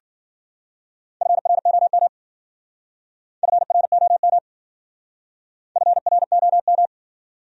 Morse code, heard as a keyed tone, about 49 wpm, 700 Hz. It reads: FROM FROM FROM
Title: Speed-Racing Top 100 Words 40wpm